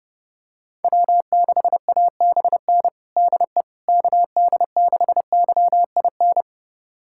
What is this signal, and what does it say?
Morse code, heard as a keyed tone, about 32 wpm, 700 Hz. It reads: W6ABN DE KD6YIN
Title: USA Callsign DE Callsign 30wpm